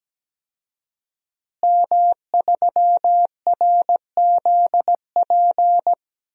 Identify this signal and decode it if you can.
Morse code, heard as a keyed tone, about 17 wpm, 700 Hz. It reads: M3RZP
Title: DXCC-100 Call Signs - 5 Characters - Repeated 2x - 17wpm